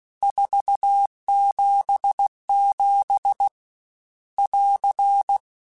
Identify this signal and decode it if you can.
Morse code, heard as a keyed tone, about 16 wpm, 800 Hz. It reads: 477 <AR>
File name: RSTs / RST-mp3s / Your RSTs @15